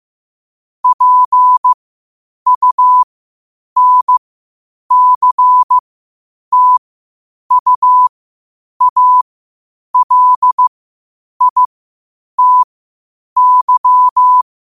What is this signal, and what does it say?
Morse code, heard as a keyed tone, about 15 wpm, spaced as for 8 wpm, 1000 Hz. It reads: PUNCTUALITY
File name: qotdfiles / 8wpm